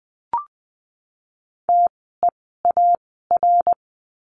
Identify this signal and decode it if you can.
Morse code, heard as a keyed tone, about 20 wpm, spaced as for 14 wpm, 700 Hz. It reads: TEAR
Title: 054 Lesson 10f - Letter r - Words 20wpm